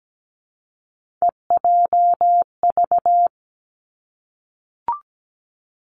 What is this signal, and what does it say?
Morse code, heard as a keyed tone, about 17 wpm, 700 Hz. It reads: EJV
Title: Three Challenging Letter-Number Combinations 17wpm